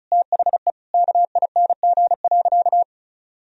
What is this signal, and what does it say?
Morse code, heard as a keyed tone, about 36 wpm, 700 Hz. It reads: THE KING.
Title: Sentences from Top 1000 Words - Review Part 4 - 35wpm